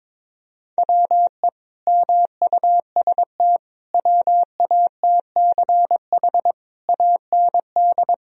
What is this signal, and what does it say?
Morse code, heard as a keyed tone, about 22 wpm, 700 Hz. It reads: WE MUST WATCH AND
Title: Sentences from Top 1000 Words - Review Part 1 - 22wpm